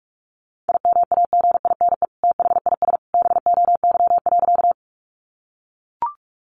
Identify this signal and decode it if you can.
Morse code, heard as a keyed tone, about 44 wpm, 700 Hz. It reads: IMAGINE THIS BOY.
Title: Sentences from Top 1400 Words 45wpm